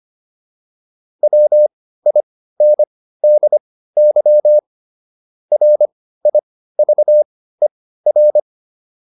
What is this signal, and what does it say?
Morse code, heard as a keyed tone, about 25 wpm, spaced as for 15 wpm, 600 Hz. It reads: WINDY RIVER